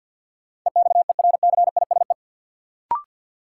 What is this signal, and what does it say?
Morse code, heard as a keyed tone, about 53 wpm, 700 Hz. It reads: EXERCISE
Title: Sentences from Top 1200 Words - New Word Drill Alt - 50wpm